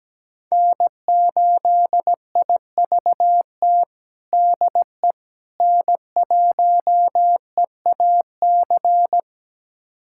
Morse code, heard as a keyed tone, about 17 wpm, 700 Hz. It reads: N8IVT DE N1EAC
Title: USA Callsign DE Callsign - 5 Characters - 17wpm